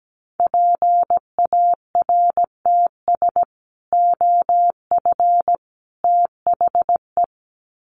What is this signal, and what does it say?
Morse code, heard as a keyed tone, about 17 wpm, 700 Hz. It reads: PARTS OF THE
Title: Sets of 3 Words 17wpm